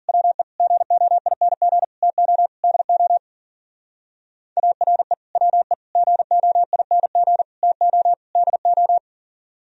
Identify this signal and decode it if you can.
Morse code, heard as a keyed tone, about 50 wpm, 700 Hz. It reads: WE GOING TO DO ARE WE GOING TO DO
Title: Speed-Racing - Sets of 5 Words 40wpm